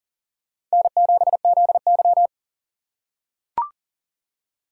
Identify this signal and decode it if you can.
Morse code, heard as a keyed tone, about 40 wpm, 700 Hz. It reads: N7ZY
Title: USA Call Signs - 4 Characters - 6x Character Spacing - 40wpm